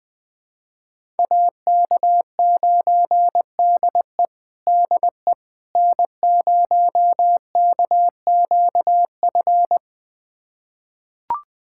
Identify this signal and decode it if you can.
Morse code, heard as a keyed tone, about 21 wpm, 700 Hz. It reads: AK9DE DE N0KQF
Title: USA Callsign DE Callsign - 5 Characters - 20wpm